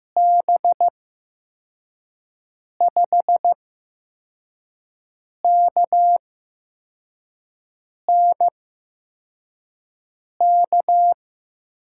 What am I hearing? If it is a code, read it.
Morse code, heard as a keyed tone, about 15 wpm, spaced as for 4 wpm, 700 Hz. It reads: B5KNK
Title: USA Call Signs - 8x Character Spacing - 15wpm